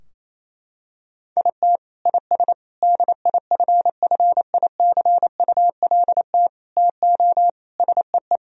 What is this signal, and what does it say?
Morse code, heard as a keyed tone, about 28 wpm, 700 Hz. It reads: IT IS DIFFICULT TO SEE
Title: Sentences from Top 1000 Words - Review Part 4 - 28wpm